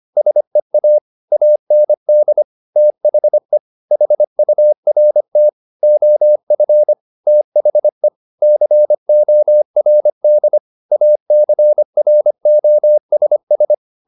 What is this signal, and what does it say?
Morse code, heard as a keyed tone, about 25 wpm, 600 Hz. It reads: SEA AND THE HURT OF THE CORD ACROSS
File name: the old man and the sea part 1 25wpm